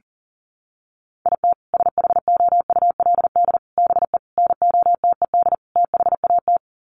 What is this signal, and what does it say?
Morse code, heard as a keyed tone, about 37 wpm, 700 Hz. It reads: IT SHOULD BE NOTED THAT
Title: Sets of 5 Words 40wpm